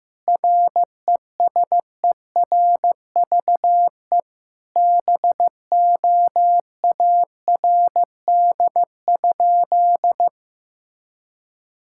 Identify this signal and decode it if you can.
Morse code, heard as a keyed tone, about 16 wpm, 700 Hz. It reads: RESERVE BOARD?
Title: Sentences from Top 1000 Words - Review Part 6 - 15wpm